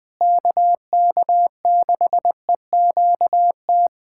Morse code, heard as a keyed tone, about 21 wpm, 700 Hz. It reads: KK6EQT
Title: USA Callsign DE Callsign - 6 Characters - 20wpm